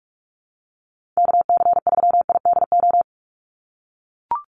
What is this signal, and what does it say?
Morse code, heard as a keyed tone, about 44 wpm, 700 Hz. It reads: KC3IDO
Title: USA Call Signs 45wpm